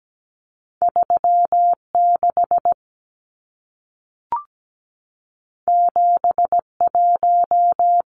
Morse code, heard as a keyed tone, about 17 wpm, 700 Hz. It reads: 36 71
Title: Numbers - 2-Digit 17wpm